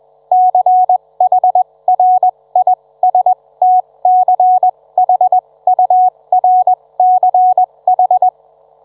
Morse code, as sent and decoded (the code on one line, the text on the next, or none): -.-. .... .-. .. ... - -.-. .... ..- .-. -.-. ....
CHRISTCHURCH